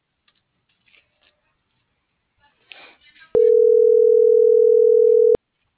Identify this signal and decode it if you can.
Ringback tone (North America), 440 + 480 Hz.